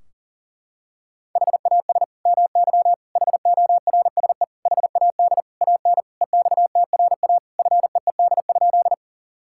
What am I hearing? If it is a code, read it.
Morse code, heard as a keyed tone, about 40 wpm, 700 Hz. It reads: HAS MY HORSE HAD AN EXTRA FEED?
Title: Sentences from Top 1600 Words 40wpm